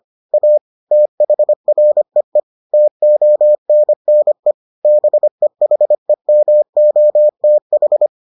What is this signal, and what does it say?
Morse code, heard as a keyed tone, about 25 wpm, 600 Hz. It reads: A THREE TONNE BEHEMOTH